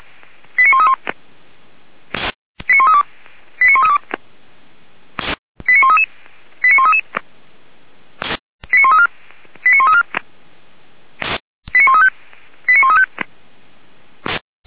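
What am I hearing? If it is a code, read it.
Five-tone selective call (ZVEI), heard as quick runs of five tones, tones of about 80 ms. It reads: 89131 89132 89132 89133 89133 89134 89134 89135 89135
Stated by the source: Five-tone ZVEI selective call received by SDR.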